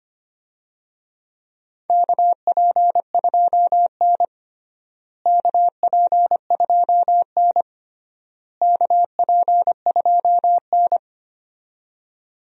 Morse code, heard as a keyed tone, about 25 wpm, 700 Hz. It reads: KP2N KP2N KP2N
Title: DXCC-100 Call Signs - 4 Characters - Repeated 3x - 25wpm